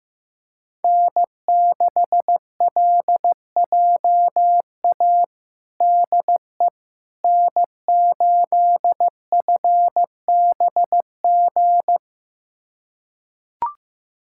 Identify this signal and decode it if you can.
Morse code, heard as a keyed tone, about 15 wpm, 700 Hz. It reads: N6LJA DE N8FBG
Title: USA Callsign DE Callsign - 5 Characters - 15wpm